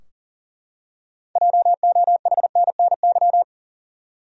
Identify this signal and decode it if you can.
Morse code, heard as a keyed tone, about 40 wpm, 700 Hz. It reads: JOHNNY
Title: USA Common Names - 40wpm